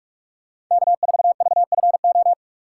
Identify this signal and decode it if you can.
Morse code, heard as a keyed tone, about 46 wpm, 700 Hz. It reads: K4VFO
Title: USA Call Signs - 5 Characters - 4x Character Spacing - 45wpm